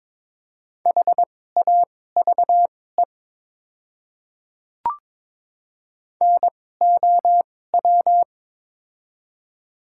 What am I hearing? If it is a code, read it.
Morse code, heard as a keyed tone, about 22 wpm, spaced as for 16 wpm, 700 Hz. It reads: HAVE NOW
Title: 2x Character Spacing - Top 100 Words - 22wpm